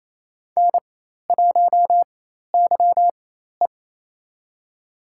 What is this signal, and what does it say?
Morse code, heard as a keyed tone, about 28 wpm, spaced as for 13 wpm, 700 Hz. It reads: N1YE
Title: USA Call Signs - 4x Character Spacing - 28wpm